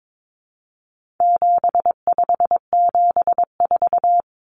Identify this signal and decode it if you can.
Morse code, heard as a keyed tone, about 22 wpm, 700 Hz. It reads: 7574
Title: Common Cut Numbers - 4-Digit 22wpm